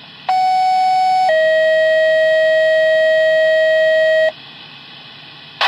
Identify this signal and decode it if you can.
Two-tone page, heard as a one-second tone, then a three-second tone, about 750 then 640 Hz, with noise. It is heard from 0.3 to 4.3 s.